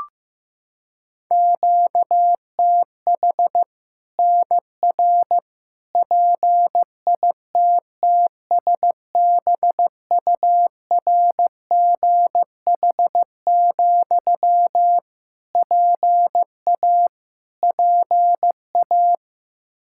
Morse code, heard as a keyed tone, about 15 wpm, 700 Hz. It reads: QTH NR PITTSBURGH, PA PA
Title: Practice QSO - Exchange 1 - Beginning - 15wpm